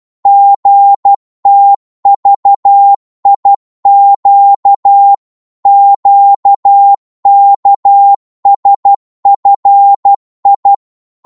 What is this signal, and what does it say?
Morse code, heard as a keyed tone, about 12 wpm, 800 Hz. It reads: GTVIQ QKSFI
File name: Morse / ltr-12wpm